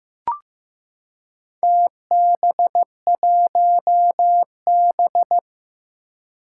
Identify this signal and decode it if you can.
Morse code, heard as a keyed tone, about 15 wpm, 700 Hz. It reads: TB1B
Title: Four Challenging Letter-Number Combinations 15wpm